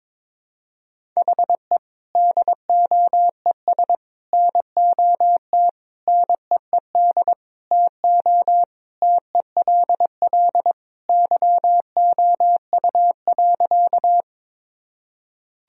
Morse code, heard as a keyed tone, about 21 wpm, 700 Hz. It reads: HE DOES NOT NEED TO TELL YOU.
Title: Sentences from Top 1000 Words - Review Part 1 - 22wpm